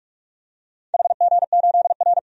Morse code, heard as a keyed tone, about 42 wpm, 700 Hz. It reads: HG8R
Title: DXCC-100 Call Signs - 4 Characters - 45wpm